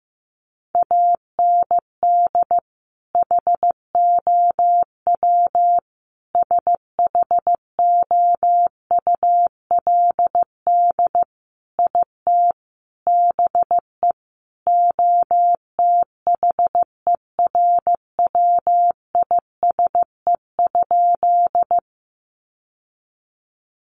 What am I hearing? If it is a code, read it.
Morse code, heard as a keyed tone, about 15 wpm, 700 Hz. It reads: AND HOW SHOULD IT BE OTHERWISE?
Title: Sentences from Top 1200 Words 15wpm